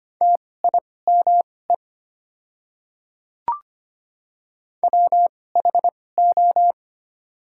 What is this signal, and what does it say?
Morse code, heard as a keyed tone, about 25 wpm, spaced as for 18 wpm, 700 Hz. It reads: TIME WHO
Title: 2x Character Spacing - Top 100 Words - 25wpm